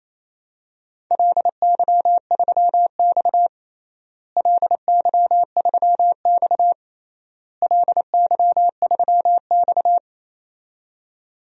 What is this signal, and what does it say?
Morse code, heard as a keyed tone, about 28 wpm, 700 Hz. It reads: LY3X LY3X LY3X
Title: DXCC-100 Call Signs - 4 Characters - Repeated 3x - 28wpm